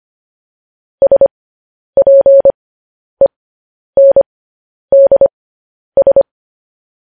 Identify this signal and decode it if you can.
Morse code, heard as a keyed tone, about 25 wpm, spaced as for 10 wpm, 550 Hz. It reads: SPENDS
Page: news headlines in morse code at 10 wpm